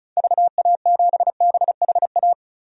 Morse code, heard as a keyed tone, about 36 wpm, 700 Hz. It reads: VA7BHA
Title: DXCC-100 Call Signs - 6 Characters - 35wpm